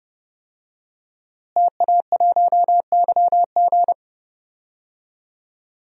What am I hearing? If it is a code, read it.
Morse code, heard as a keyed tone, about 29 wpm, 700 Hz. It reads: TA1YG
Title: DXCC-100 Call Signs - 5 Characters - Repeated 3x - 30wpm